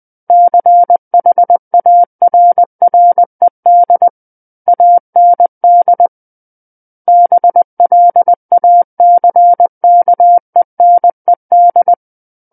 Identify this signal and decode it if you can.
Morse code, heard as a keyed tone, about 20 wpm, 700 Hz. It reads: CHARRED AND BLACKENED
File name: 20 WPM